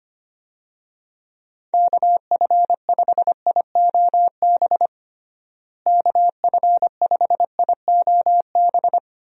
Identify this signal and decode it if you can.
Morse code, heard as a keyed tone, about 25 wpm, 700 Hz. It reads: KF5IOB KF5IOB
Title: USA Call Signs - 6 Characters - Repeated 2x - 25wpm